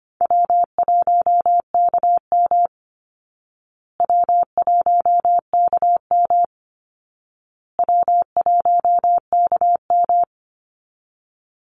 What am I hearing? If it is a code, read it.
Morse code, heard as a keyed tone, about 25 wpm, 700 Hz. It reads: W1KM W1KM W1KM
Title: Easy Call Signs Repeated 3x - 25wpm